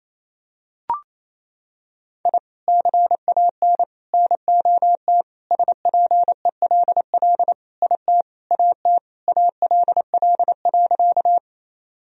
Morse code, heard as a keyed tone, about 28 wpm, 700 Hz. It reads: I CAN NOT SPELL IT AT ALL.
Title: Sentences from Top 1000 Words - Review Part 1 - 28wpm